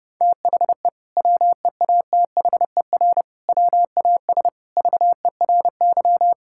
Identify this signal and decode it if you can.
Morse code, heard as a keyed tone, about 30 wpm, 700 Hz. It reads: THE WEATHER WAS VERY
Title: Sentences from Top 1000 Words - Review Part 5 - 30wpm